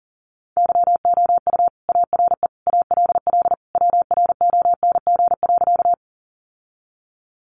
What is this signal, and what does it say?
Morse code, heard as a keyed tone, about 40 wpm, 700 Hz. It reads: YOU ARE ALL WRONG.
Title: Sentences from Top 900 Words 40wpm